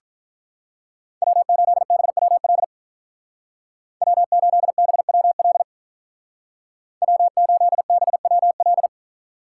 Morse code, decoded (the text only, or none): W8BWL W8BWL W8BWL